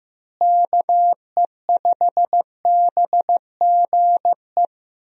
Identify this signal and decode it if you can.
Morse code, heard as a keyed tone, about 15 wpm, 700 Hz. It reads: KE5BGE
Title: USA Call Signs - 6 Characters - Repeated 2x - 15wpm